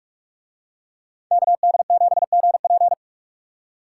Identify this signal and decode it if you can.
Morse code, heard as a keyed tone, about 44 wpm, 700 Hz. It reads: KD7GP